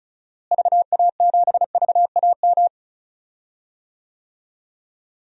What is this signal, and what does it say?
Morse code, heard as a keyed tone, about 34 wpm, 700 Hz. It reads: VA7VAM
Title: DXCC-100 Call Signs - 6 Characters - Repeated 2x - 35wpm